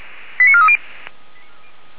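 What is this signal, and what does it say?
Five-tone selective call (ZVEI), heard as quick runs of five tones, tones of about 70 ms. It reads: 89430
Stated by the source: Five-tone ZVEI selective call received on an SDR, from the fire brigade Feuerwehr Orlamünde audio server.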